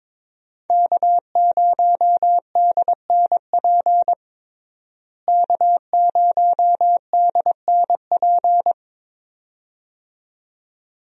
Morse code, heard as a keyed tone, about 22 wpm, 700 Hz. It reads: K0DNP K0DNP
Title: USA Call Signs - 5 Characters - Repeated 2x - 22wpm